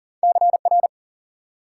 Morse code, heard as a keyed tone, about 54 wpm, spaced as for 47 wpm, 700 Hz. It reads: CR
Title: Speed-Racing Multi-letter Phonemes - 40wpm